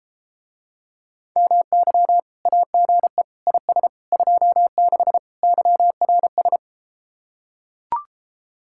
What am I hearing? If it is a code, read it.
Morse code, heard as a keyed tone, about 33 wpm, 700 Hz. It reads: MY AGE IS 26 YRS